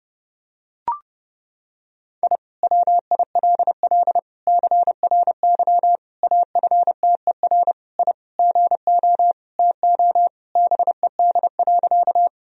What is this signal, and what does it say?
Morse code, heard as a keyed tone, about 30 wpm, 700 Hz. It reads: I WILL CRY AFTER I GO TO BED.
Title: Sentences from Top 1000 Words - Review Part 2 - 30wpm